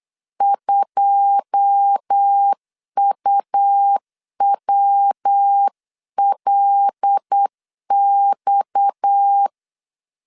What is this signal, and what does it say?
Morse code, heard as a keyed tone, about 8 wpm, 800 Hz. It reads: 2UWLX
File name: CW / DET / 1 9 1500C S B 8wpm 5624